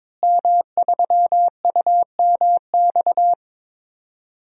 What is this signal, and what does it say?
Morse code, heard as a keyed tone, about 22 wpm, 700 Hz. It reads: M3UMX
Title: DXCC-100 Call Signs - 5 Characters - Repeated 3x - 22wpm